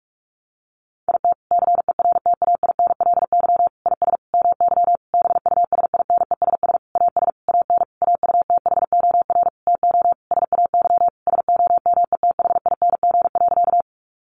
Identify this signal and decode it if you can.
Morse code, heard as a keyed tone, about 44 wpm, 700 Hz. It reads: IT CERTAINLY IS MY BUSINESS AS AN AUTHOR TO SAY SOMETHING.
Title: Sentences from Top 1200 Words 45wpm